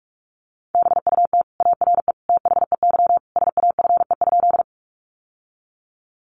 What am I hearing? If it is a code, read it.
Morse code, heard as a keyed tone, about 44 wpm, 700 Hz. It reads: BUT ARE THEY SAFE?
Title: Sentences from Top 900 Words 45wpm